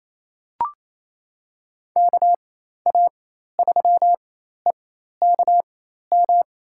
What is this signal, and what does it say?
Morse code, heard as a keyed tone, about 28 wpm, spaced as for 13 wpm, 700 Hz. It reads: KA3EKM
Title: USA Call Signs - 4x Character Spacing - 28wpm